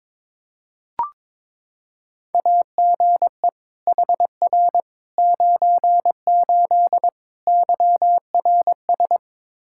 Morse code, heard as a keyed tone, about 22 wpm, 700 Hz. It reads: AGE HR 98 YRS
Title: K1USN SST QSO Practice 22wpm